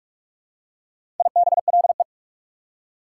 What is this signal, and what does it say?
Morse code, heard as a keyed tone, about 44 wpm, 690 Hz. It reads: IBLE
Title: Prefixes and Suffixes 45wpm